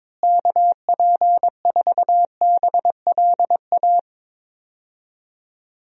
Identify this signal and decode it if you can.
Morse code, heard as a keyed tone, about 22 wpm, 700 Hz. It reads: KP4BLA
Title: DXCC-100 Call Signs - 6 Characters - Repeated 3x - 22wpm